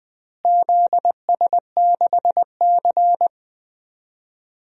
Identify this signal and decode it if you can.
Morse code, heard as a keyed tone, about 20 wpm, 700 Hz. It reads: ZS6C
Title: DXCC-100 Call Signs - 4 Characters - Repeated 3x - 20wpm